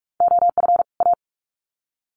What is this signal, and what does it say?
Morse code, heard as a keyed tone, about 44 wpm, 700 Hz. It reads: OF A
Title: Sets of 2 Words 45wpm